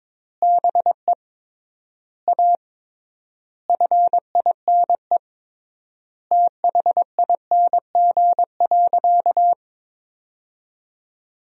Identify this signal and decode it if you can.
Morse code, heard as a keyed tone, about 22 wpm, 700 Hz. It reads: BE A FINE THING.
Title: 3x Word-Spacing - Sentences from Top 500 Words - 22wpm